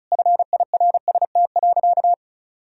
Morse code, heard as a keyed tone, about 36 wpm, 700 Hz. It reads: FIRST.
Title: Sentences from Top 1000 Words - Review Part 1 - 35wpm